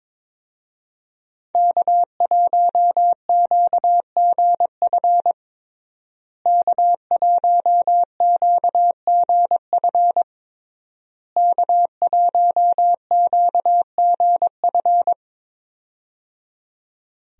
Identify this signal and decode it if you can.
Morse code, heard as a keyed tone, about 22 wpm, 700 Hz. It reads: K1QGF K1QGF K1QGF
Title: USA Call Signs - 5 Characters - Repeated 3x - 22wpm